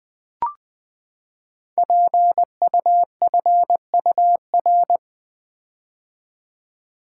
Morse code, heard as a keyed tone, about 20 wpm, 700 Hz. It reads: PUFUR
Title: Code Groups - 5 Letters - 20wpm